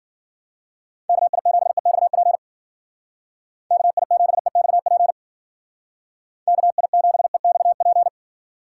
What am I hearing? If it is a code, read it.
Morse code, heard as a keyed tone, about 53 wpm, 700 Hz. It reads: KI7EXP KI7EXP KI7EXP